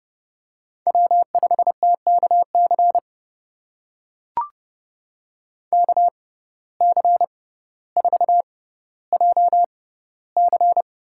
Morse code, heard as a keyed tone, about 32 wpm, spaced as for 16 wpm, 700 Hz. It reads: W5TKC KC4JC
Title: USA Call Signs - 5 Characters - 6x Character Spacing - 30wpm